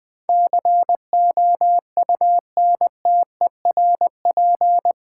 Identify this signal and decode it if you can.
Morse code, heard as a keyed tone, about 20 wpm, 700 Hz. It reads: COUNTERP
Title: Build A Word - 4 Letters At A Time - 11-Letter Words - 20wpm